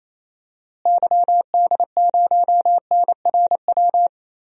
Morse code, heard as a keyed tone, about 28 wpm, 700 Hz. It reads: YD0NRW
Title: DXCC-100 Call Signs - 6 Characters - Repeated 3x - 28wpm